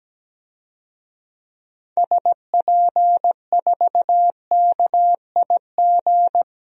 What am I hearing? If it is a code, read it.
Morse code, heard as a keyed tone, about 17 wpm, 700 Hz. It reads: SP4KIG
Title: DXCC-100 Call Signs - 6 Characters - 17wpm